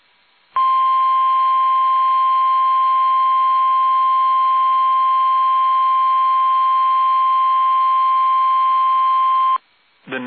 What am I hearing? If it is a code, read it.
Emergency Alert System (SAME) tones (the 1050 Hz attention tone). It is heard from 0.5 to 9.6 s.
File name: eas / KUSO / KUSO